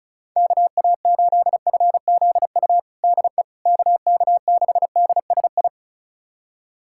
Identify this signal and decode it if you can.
Morse code, heard as a keyed tone, about 36 wpm, 690 Hz. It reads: KA8FZU DE KK6DSI
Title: USA Callsign DE Callsign - 6 Characters - 35wpm